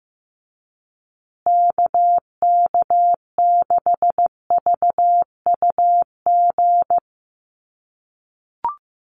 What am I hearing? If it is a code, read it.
Morse code, heard as a keyed tone, about 15 wpm, 700 Hz. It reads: KK6VUG
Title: USA Call Signs Repeated 3x 15wpm